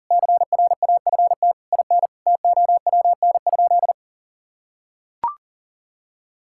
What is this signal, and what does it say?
Morse code, heard as a keyed tone, about 41 wpm, 700 Hz. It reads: CRAFT IN TOWN?
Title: Sentences from Top 1000 Words - Review Part 6 - 40wpm